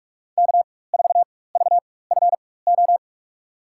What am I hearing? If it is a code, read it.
Morse code, heard as a keyed tone, about 47 wpm, spaced as for 21 wpm, 700 Hz. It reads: K4VFO
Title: USA Call Signs - 4x Character Spacing - 45wpm